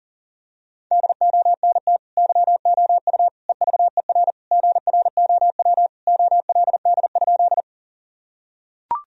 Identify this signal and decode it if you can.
Morse code, heard as a keyed tone, about 40 wpm, 700 Hz. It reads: DONT YOU EVER GROW OLD?
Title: Sentences from Top 1000 Words - Review Part 1 - 40wpm